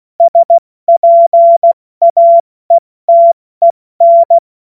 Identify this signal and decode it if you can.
Morse code, heard as a keyed tone, about 16 wpm, 680 Hz. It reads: SPAETEN
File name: Witz-16-September-2025-70-F80